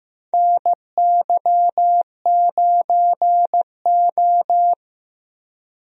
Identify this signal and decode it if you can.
Morse code, heard as a keyed tone, about 15 wpm, 700 Hz. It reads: NY9O
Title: USA Call Signs - 4 Characters - Repeated 3x - 15wpm